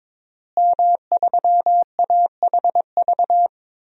Morse code, heard as a keyed tone, about 22 wpm, 700 Hz. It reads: M3AHV
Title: DXCC-100 Call Signs - 5 Characters - Repeated 2x - 22wpm